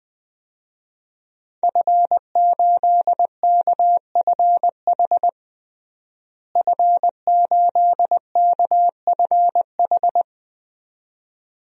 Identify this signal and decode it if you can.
Morse code, heard as a keyed tone, about 21 wpm, 700 Hz. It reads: F8KFH F8KFH
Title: DXCC-100 Call Signs - 5 Characters - Repeated 2x - 20wpm